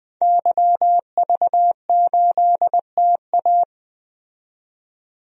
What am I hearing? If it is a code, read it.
Morse code, heard as a keyed tone, about 20 wpm, 700 Hz. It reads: YV8TA
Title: DXCC-100 Call Signs - 5 Characters - 20wpm